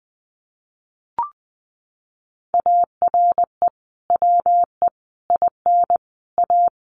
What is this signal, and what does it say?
Morse code, heard as a keyed tone, about 20 wpm, 700 Hz. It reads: ARE WE IN A
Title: Sentences from Top 1300 Words 20wpm